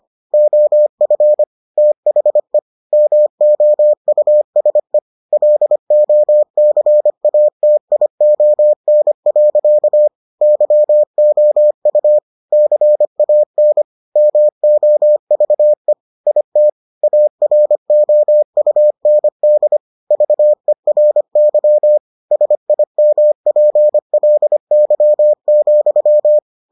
Morse code, heard as a keyed tone, about 25 wpm, 600 Hz. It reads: OF THE MOUSE LOCATION. YOU CAN MOVE IT AROUND VERY SIMPLY,